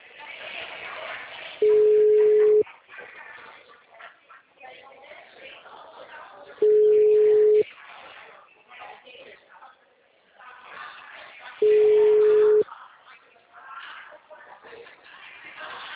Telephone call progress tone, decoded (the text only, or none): ringback tone